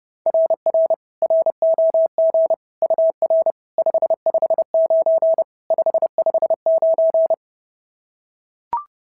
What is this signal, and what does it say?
Morse code, heard as a keyed tone, about 32 wpm, 690 Hz. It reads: RR ROG UR 559 559
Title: SOTA QSO Practice 30wpm